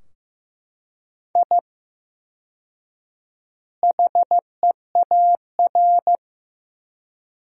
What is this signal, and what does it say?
Morse code, heard as a keyed tone, about 15 wpm, 700 Hz. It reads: I HEAR
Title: 4x Word-Spacing - Sentences from Top 500 Words - 15wpm